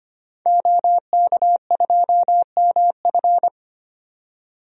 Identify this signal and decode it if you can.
Morse code, heard as a keyed tone, about 25 wpm, 690 Hz. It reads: OK2MF
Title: DXCC-100 Call Signs - 5 Characters - 25wpm